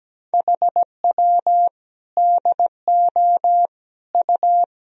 Morse code, heard as a keyed tone, about 17 wpm, 700 Hz. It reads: HW DO U
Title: Sentences from Top 400 Words and Abbreviations 17wpm